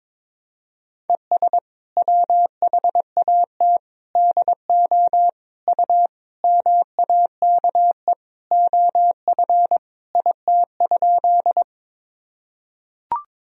Morse code, heard as a keyed tone, about 21 wpm, 700 Hz. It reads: ES WHAT DO U MAKE OF IT?
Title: Sentences from Top 100 Words and Abbreviations 22wpm